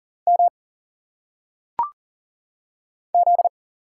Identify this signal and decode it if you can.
Morse code, heard as a keyed tone, about 44 wpm, 700 Hz. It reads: M Z
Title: Single Letters - Phonetic Pronunciation 40wpm